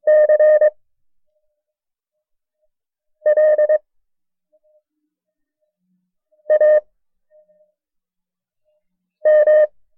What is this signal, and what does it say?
Morse code, heard as a keyed tone, about 23 wpm, spaced as for 3 wpm, 610 Hz. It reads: CLAM
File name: FreeFiles / CW-practice / KB6MT-tape1-side2